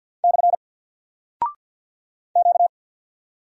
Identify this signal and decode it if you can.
Morse code, heard as a keyed tone, about 50 wpm, 700 Hz. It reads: / Q
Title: Single Character (Letter:Number:Punctuation:Prosign) 50wpm